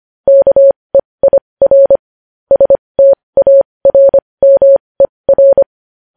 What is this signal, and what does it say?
Morse code, heard as a keyed tone, about 25 wpm, spaced as for 20 wpm, 550 Hz. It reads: KEIR STARMER
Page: news-headlines-in-morse-code-at-20-wpm